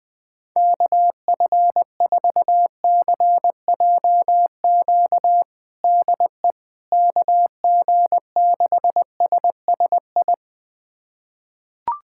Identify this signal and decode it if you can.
Morse code, heard as a keyed tone, about 20 wpm, 700 Hz. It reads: KF4CJQ DE KG6SSI
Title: USA Callsign DE Callsign - 6 Characters - 20wpm